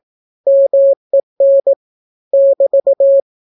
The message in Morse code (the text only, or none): MEN <BT>